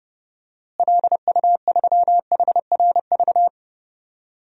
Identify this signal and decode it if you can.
Morse code, heard as a keyed tone, about 31 wpm, 700 Hz. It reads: LU3HRV
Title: DXCC-100 Call Signs - 6 Characters - Repeated 3x - 30wpm